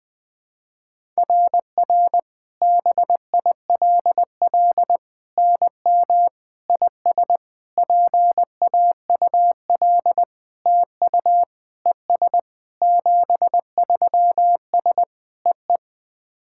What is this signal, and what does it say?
Morse code, heard as a keyed tone, about 19 wpm, 700 Hz. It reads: RR BILL NM IS PAUL TU ES 73S EE
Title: SOTA QSO Practice 20wpm